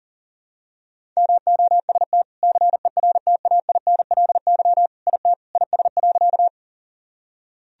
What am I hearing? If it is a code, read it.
Morse code, heard as a keyed tone, about 37 wpm, 700 Hz. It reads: MOST CERTAINLY IT IS.
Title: Sentences from Top 1000 Words - Review Part 1 - 40wpm